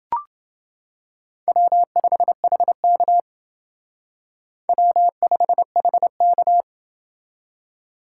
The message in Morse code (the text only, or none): W5HK W5HK